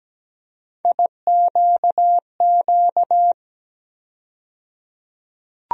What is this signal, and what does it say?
Morse code, heard as a keyed tone, about 17 wpm, 700 Hz. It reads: IQQ
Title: USA Call Sign Postfixes - 3 Letters 17wpm